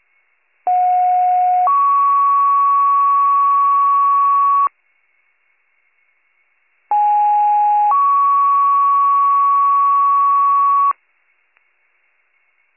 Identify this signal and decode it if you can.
Two-tone page, heard as a one-second tone, then a three-second tone, about 730 then 1120 Hz, about 830 then 1120 Hz. It is heard from 0.7 to 10.9 s.